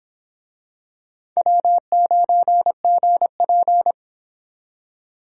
Morse code, heard as a keyed tone, about 24 wpm, 700 Hz. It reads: W9GP